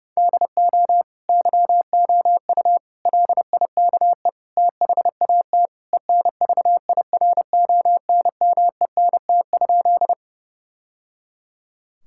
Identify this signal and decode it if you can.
Morse code, heard as a keyed tone, about 30 wpm, 700 Hz. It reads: DO YOU LIKE THAT ENVIRONMENT?
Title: Sentences from Top 1000 Words - Review Part 5 - 30wpm